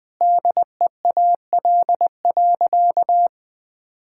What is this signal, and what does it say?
Morse code, heard as a keyed tone, about 21 wpm, 700 Hz. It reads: DEAL.
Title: Sentences with 8-Word Phrases - 20wpm